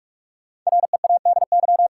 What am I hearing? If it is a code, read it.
Morse code, heard as a keyed tone, about 46 wpm, 700 Hz. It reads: READY
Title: Sentences from Top 400 Words - New Word Drill - 45wpm